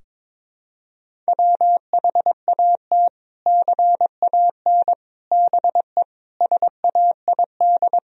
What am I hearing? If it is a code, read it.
Morse code, heard as a keyed tone, about 22 wpm, 700 Hz. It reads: WHAT CAN BE SAID
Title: Sentences from Top 1000 Words - Review Part 4 - 22wpm